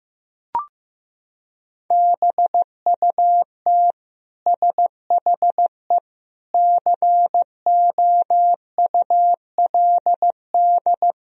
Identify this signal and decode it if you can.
Morse code, heard as a keyed tone, about 15 wpm, 700 Hz. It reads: BUT SHE COULD
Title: Sentences from Top 1000 Words - Review Part 4 - 15wpm